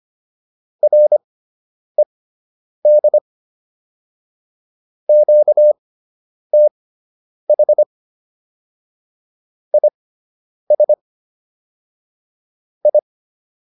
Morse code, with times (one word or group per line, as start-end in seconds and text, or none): 0.8-3.2 RED
5.1-7.8 QTH
9.7-10.9 IS
12.8-13.0 I